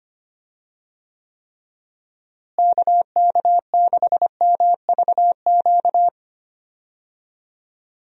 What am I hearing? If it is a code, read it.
Morse code, heard as a keyed tone, about 25 wpm, 700 Hz. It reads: KK6MVQ
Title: USA Call Signs - 6 Characters - Repeated 3x - 25wpm